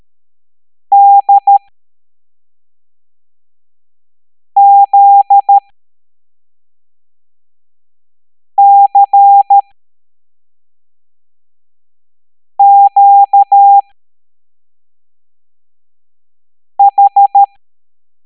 Morse code, heard as a keyed tone, about 13 wpm, spaced as for 3 wpm, 800 Hz. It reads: DZCQH
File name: hamdata / L09-EX04